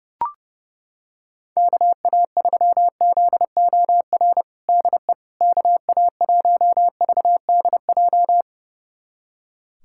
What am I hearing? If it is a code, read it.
Morse code, heard as a keyed tone, about 30 wpm, 700 Hz. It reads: KA3ZOR DE KA1VDJ
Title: USA Callsign DE Callsign - 6 Characters - 30wpm